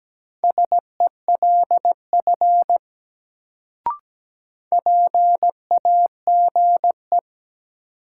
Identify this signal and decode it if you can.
Morse code, heard as a keyed tone, about 17 wpm, 700 Hz. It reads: SELF PAGE
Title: Sentences from Top 1000 Words - Review Part 1 - New Word Drill - 17wpm